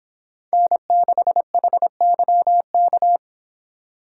Morse code, heard as a keyed tone, about 26 wpm, 700 Hz. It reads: N6HYK